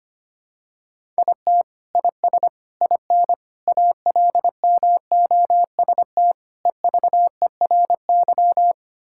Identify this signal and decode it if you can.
Morse code, heard as a keyed tone, about 25 wpm, 700 Hz. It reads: IT IS IN ALMOST EVERY
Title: Sentences from Top 1000 Words - Review Part 5 - 25wpm